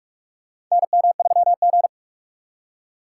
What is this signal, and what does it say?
Morse code, heard as a keyed tone, about 42 wpm, 700 Hz. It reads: NM3G
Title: USA Call Signs - 4 Characters - 3x Character Spacing - 45wpm